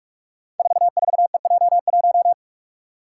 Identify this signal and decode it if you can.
Morse code, heard as a keyed tone, about 44 wpm, 690 Hz. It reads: 44EJ1
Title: Code Groups - 5 Letter:Numbers - 45wpm